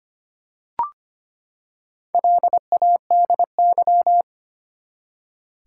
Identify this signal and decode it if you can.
Morse code, heard as a keyed tone, about 25 wpm, 700 Hz. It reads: LADY
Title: Four Letter Words - 25wpm